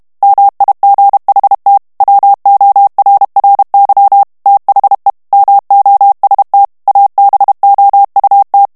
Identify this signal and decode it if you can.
Morse code, heard as a keyed tone, about 32 wpm, 800 Hz. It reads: MIGHT WORRY THE MOST ABOUT
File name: morse mp3 / ANGLAIS texte clair 1 vitesse 30 mots minute